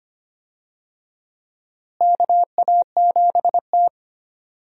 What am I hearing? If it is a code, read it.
Morse code, heard as a keyed tone, about 25 wpm, 700 Hz. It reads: KA7T
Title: USA Call Signs - 4 Characters - Repeated 2x - 25wpm